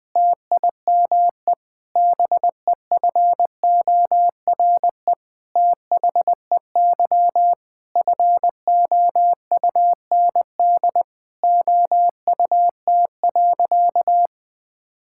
Morse code, heard as a keyed tone, about 21 wpm, 700 Hz. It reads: TIME BEFORE THEY FOUND OUT.
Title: Sentences with 8-Word Phrases - 20wpm